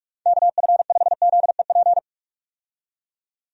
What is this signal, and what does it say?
Morse code, heard as a keyed tone, about 46 wpm, 700 Hz. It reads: KF5ZEP
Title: USA Call Signs - 6 Characters - 45wpm